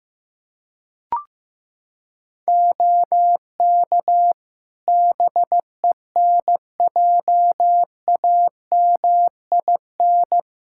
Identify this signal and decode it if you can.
Morse code, heard as a keyed tone, about 15 wpm, 700 Hz. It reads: OK BENJAMIN
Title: Practice QSO - Exchange 4 - Ending - 15wpm